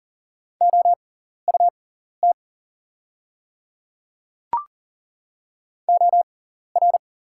Morse code, heard as a keyed tone, about 40 wpm, spaced as for 14 wpm, 700 Hz. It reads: OUT OR
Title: 6x Character Spacing - Top 100 Words - 40wpm